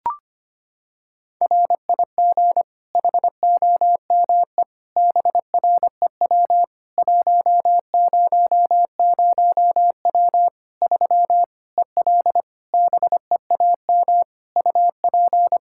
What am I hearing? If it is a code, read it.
Morse code, heard as a keyed tone, about 25 wpm, 700 Hz. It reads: RIG HOME BREW 100W 3 EL BEAM UP
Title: Practice QSO - Exchange 2 - Info - 25wpm